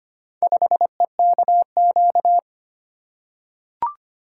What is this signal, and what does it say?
Morse code, heard as a keyed tone, about 25 wpm, 700 Hz. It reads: 5EKQ
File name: practice / 25wpm / Four Letter-Number Combinations - 25wpm